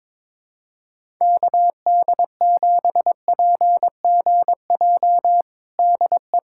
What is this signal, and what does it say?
Morse code, heard as a keyed tone, about 22 wpm, 700 Hz. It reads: KD7PGJ DE
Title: USA Callsign DE Callsign 22wpm